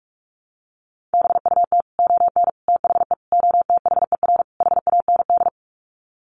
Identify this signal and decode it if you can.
Morse code, heard as a keyed tone, about 44 wpm, 700 Hz. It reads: BUT ON THE OTHER HAND
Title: Sets of 5 Words 45wpm